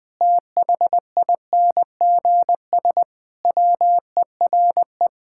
Morse code, heard as a keyed tone, about 20 wpm, 700 Hz. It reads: THINGS WERE
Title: Sentences with 5-Word Phrases - 20wpm